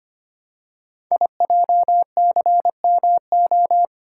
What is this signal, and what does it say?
Morse code, heard as a keyed tone, about 26 wpm, 700 Hz. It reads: IJCMO
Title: Code Groups - 5 Letters - 25wpm